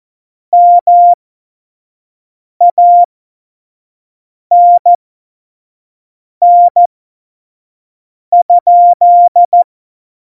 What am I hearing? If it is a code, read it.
Morse code, heard as a keyed tone, about 14 wpm, spaced as for 5 wpm, 700 Hz. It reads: MANN?
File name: Morsekurs / Witze / Witz 03.Juli 2018, 25 F70